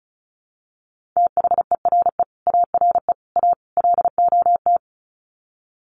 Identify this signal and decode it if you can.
Morse code, heard as a keyed tone, about 34 wpm, 700 Hz. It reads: THERE ARE A LOT
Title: Sets of 4 Words 35wpm